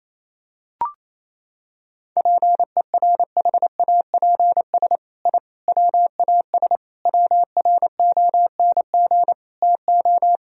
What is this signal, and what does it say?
Morse code, heard as a keyed tone, about 28 wpm, 700 Hz. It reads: PERHAPS I WAS WRONG TO
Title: Sentences from Top 1000 Words - Review Part 1 - 28wpm